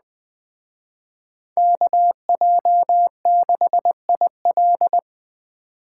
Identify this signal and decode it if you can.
Morse code, heard as a keyed tone, about 20 wpm, 700 Hz. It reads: KJ6IL
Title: Speed Racing - USA 2x2 Call Signs 17wpm